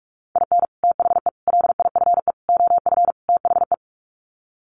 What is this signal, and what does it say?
Morse code, heard as a keyed tone, about 44 wpm, 700 Hz. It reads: IN THE LIFE OF THE
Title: Sets of 5 Words Encore2 45wpm